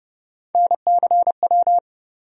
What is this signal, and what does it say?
Morse code, heard as a keyed tone, about 30 wpm, 700 Hz. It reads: NCW
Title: USA Call Sign Postfixes - 3 Letters 30wpm